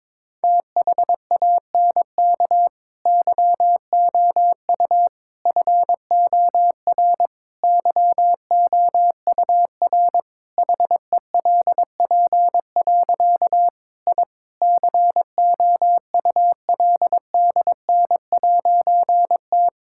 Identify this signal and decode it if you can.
Morse code, heard as a keyed tone, about 22 wpm, 700 Hz. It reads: THANK YOU FOR YOUR HELP. I COULDN'T
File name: Difficult Sentences with Hints - 22wpm